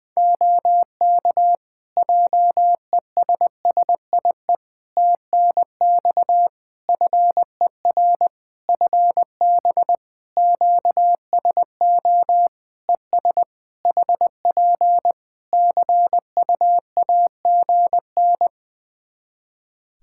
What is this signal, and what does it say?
Morse code, heard as a keyed tone, about 19 wpm, 700 Hz. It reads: OK JESSIE TNX FER FB QSO ES HP CUAGN
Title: Practice QSO - Exchange 4 - Ending - 20wpm